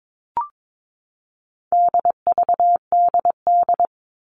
Morse code, heard as a keyed tone, about 22 wpm, 700 Hz. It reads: DVDD
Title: All Cut Numbers - 4-Digit 22wpm